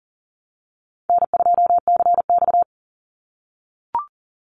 Morse code, heard as a keyed tone, about 40 wpm, 700 Hz. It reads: N2CX
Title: Easy Call Signs - 40wpm